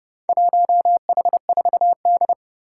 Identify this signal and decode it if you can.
Morse code, heard as a keyed tone, about 30 wpm, 700 Hz. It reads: 1H4D
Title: Four Challenging Letter-Number Combinations 30wpm